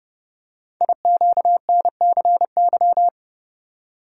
Morse code, heard as a keyed tone, about 28 wpm, 700 Hz. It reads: IQNCY
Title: Code Groups - 5 Letters - 30wpm